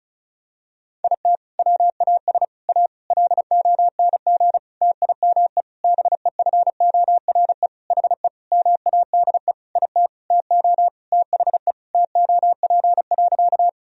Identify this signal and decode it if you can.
Morse code, heard as a keyed tone, about 36 wpm, 700 Hz. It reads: IT WAS A LONG TIME BEFORE HE MADE IT TO THE TOP.
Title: Sentences from Top 1000 Words - Review Part 2 - 35wpm